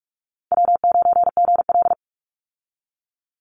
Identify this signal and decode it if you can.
Morse code, heard as a keyed tone, about 44 wpm, 700 Hz. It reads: W9GL
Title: Call Signs Repeated 3x - 45wpm